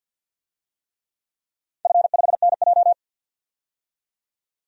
Morse code, heard as a keyed tone, about 46 wpm, 700 Hz. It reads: F5NJ
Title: DXCC-100 Call Signs - 4 Characters - 50wpm